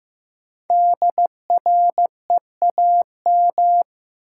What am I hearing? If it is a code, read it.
Morse code, heard as a keyed tone, about 15 wpm, 700 Hz. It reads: DREAM
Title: Sentences from Top 1500 Words - New Word Drill - 15wpm